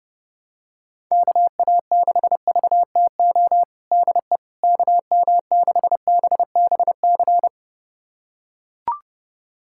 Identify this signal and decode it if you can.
Morse code, heard as a keyed tone, about 32 wpm, 690 Hz. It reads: KA6VTO DE KM6BBC
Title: USA Callsign DE Callsign 30wpm